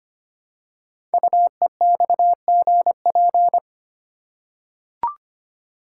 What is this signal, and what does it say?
Morse code, heard as a keyed tone, about 25 wpm, 700 Hz. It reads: UEXGP
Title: Code Groups - 5 Letters - 25wpm